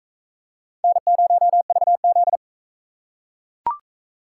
Morse code, heard as a keyed tone, about 43 wpm, 700 Hz. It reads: N0VZ